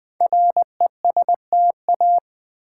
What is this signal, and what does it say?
Morse code, heard as a keyed tone, about 21 wpm, 700 Hz. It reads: RESTA
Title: Build A Word - 5 Letters At A Time - 10-Letter Words - 20wpm